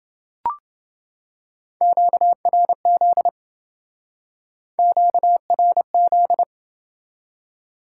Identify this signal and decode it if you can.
Morse code, heard as a keyed tone, about 28 wpm, 700 Hz. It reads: QRZ QRZ